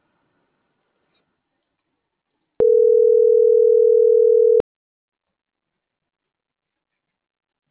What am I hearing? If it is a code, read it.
Ringback tone (North America), 440 + 480 Hz.